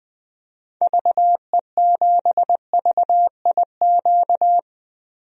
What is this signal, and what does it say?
Morse code, heard as a keyed tone, about 20 wpm, 700 Hz. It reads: VE7VIQ
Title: DXCC-100 Call Signs - 6 Characters - Repeated 2x - 20wpm